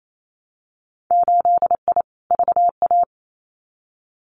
Morse code, heard as a keyed tone, about 28 wpm, 700 Hz. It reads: 8I VA
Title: Winter Field Day Exchanges Encore 28wpm